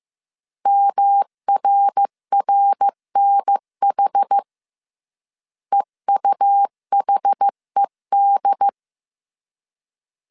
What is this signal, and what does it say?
Morse code, heard as a keyed tone, about 15 wpm, 800 Hz. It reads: MRRNH EUHED
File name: CW / DET / 1 3 1500C S C 14wpm 2458